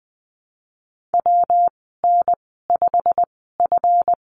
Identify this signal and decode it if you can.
Morse code, heard as a keyed tone, about 19 wpm, spaced as for 14 wpm, 700 Hz. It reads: WN5F
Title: 193 Lesson 31g - Letter v - Call Signs 20wpm